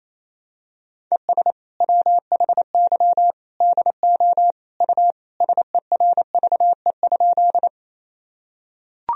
Morse code, heard as a keyed tone, about 28 wpm, 700 Hz. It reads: ES WHY DO U SERVE?
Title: Sentences from Top 400 Words and Abbreviations 28wpm